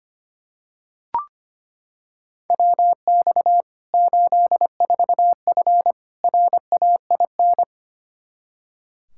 Morse code, heard as a keyed tone, about 25 wpm, 700 Hz. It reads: WX 84F RAIN
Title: K1USN SST QSO Practice 25wpm